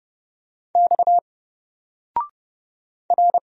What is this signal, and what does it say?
Morse code, heard as a keyed tone, about 30 wpm, 700 Hz. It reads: X R
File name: Single-Letter-Number-30wpm